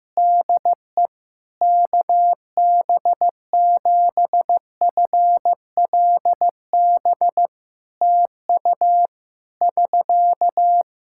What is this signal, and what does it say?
Morse code, heard as a keyed tone, about 15 wpm, 700 Hz. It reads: DE KB7FLB TU <SK>
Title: Practice QSO - Exchange 4 - Ending - 15wpm